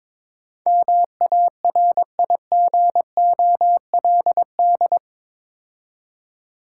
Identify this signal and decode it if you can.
Morse code, heard as a keyed tone, about 22 wpm, 690 Hz. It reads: MARIGOLD
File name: Difficult Words with Hints - 22wpm